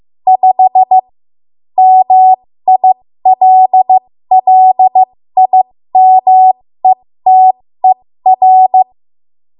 Morse code, heard as a keyed tone, about 15 wpm, spaced as for 13 wpm, 750 Hz. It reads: 5 MILLIMETER
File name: Morse / 13WPM